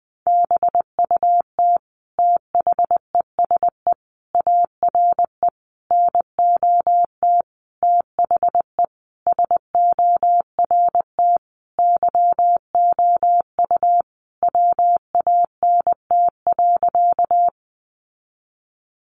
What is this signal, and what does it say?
Morse code, heard as a keyed tone, about 20 wpm, 700 Hz. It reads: BUT THESE ARE NOT THE SORT YOU WANT.
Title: Sentences from Top 1300 Words 20wpm